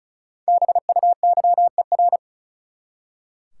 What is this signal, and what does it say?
Morse code, heard as a keyed tone, about 34 wpm, 700 Hz. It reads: BUYER
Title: Five Letter Words - 35wpm